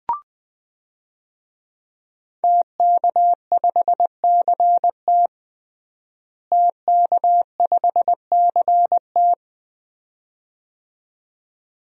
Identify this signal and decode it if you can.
Morse code, heard as a keyed tone, about 20 wpm, 700 Hz. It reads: TK5CT TK5CT
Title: DXCC-100 Call Signs - 5 Characters - Repeated 2x - 20wpm